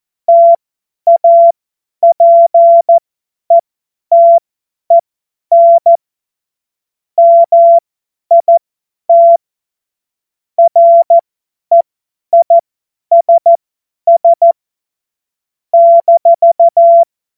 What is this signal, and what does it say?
Morse code, heard as a keyed tone, about 14 wpm, spaced as for 10 wpm, 680 Hz. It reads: TAPETEN MIT REISS -
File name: Morsekurs / Witze / Witz 04. Januar 2022, 50 F70